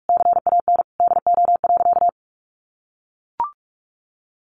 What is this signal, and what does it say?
Morse code, heard as a keyed tone, about 44 wpm, 700 Hz. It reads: CAN DO.
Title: Sentences from Top 900 Words 45wpm